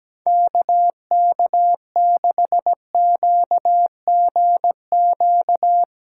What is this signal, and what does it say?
Morse code, heard as a keyed tone, about 17 wpm, 700 Hz. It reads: KK6QGQ
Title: USA Call Signs - 6 Characters - 2x Character Spacing - 17wpm